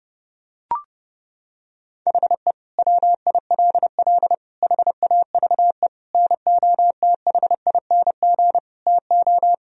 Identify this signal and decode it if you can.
Morse code, heard as a keyed tone, about 32 wpm, 700 Hz. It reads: HE WILL HAVE NOTHING TO
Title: Sentences from Top 1000 Words - Review Part 4 - 30wpm